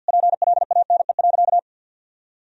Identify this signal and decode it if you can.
Morse code, heard as a keyed tone, about 51 wpm, 700 Hz. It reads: PLANE.
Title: Sentences from Top 1000 Words - Review Part 3 - 50wpm